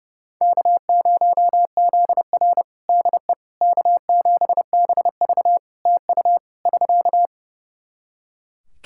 Morse code, heard as a keyed tone, about 29 wpm, 700 Hz. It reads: K0ZR DE K7BV TU <SK>
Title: Practice QSO - Exchange 4 - Ending - 30wpm